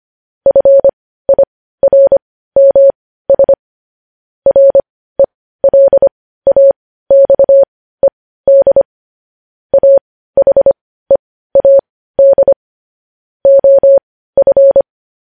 Morse code, heard as a keyed tone, about 25 wpm, spaced as for 15 wpm, 550 Hz. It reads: FIRMS RELAXED AHEAD OF
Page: news-headlines-in-morse-code-at-15-wpm